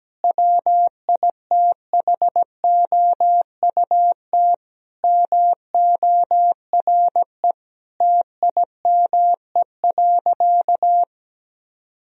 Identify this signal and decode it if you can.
Morse code, heard as a keyed tone, about 17 wpm, 700 Hz. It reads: WITHOUT MORE TIME.
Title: Sentences from Top 1000 Words - Review Part 3 - 17wpm